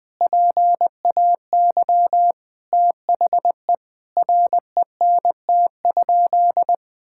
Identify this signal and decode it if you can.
Morse code, heard as a keyed tone, about 19 wpm, 700 Hz. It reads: PAY THE RENT?
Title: Sentences from Top 1000 Words - Review Part 6 - 20wpm